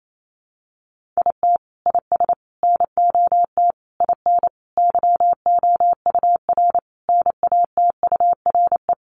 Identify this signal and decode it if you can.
Morse code, heard as a keyed tone, about 28 wpm, 700 Hz. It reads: IT IS NOT IN YOUR NATURE
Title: Sentences from Top 1200 Words 28wpm